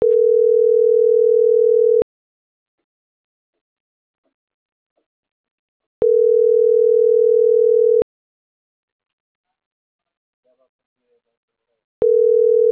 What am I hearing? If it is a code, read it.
Ringback tone (North America), 440 + 480 Hz.
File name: CENTRAL MAERSK / out carlos.carrion